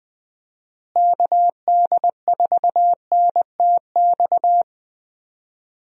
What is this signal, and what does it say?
Morse code, heard as a keyed tone, about 21 wpm, 700 Hz. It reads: KD4NTX
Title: USA Call Signs - 6 Characters - Repeated 2x - 20wpm